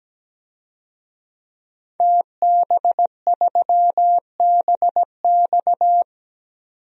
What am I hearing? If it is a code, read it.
Morse code, heard as a keyed tone, about 17 wpm, 700 Hz. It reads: TB3BX
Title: DXCC-100 Call Signs - 5 Characters - 17wpm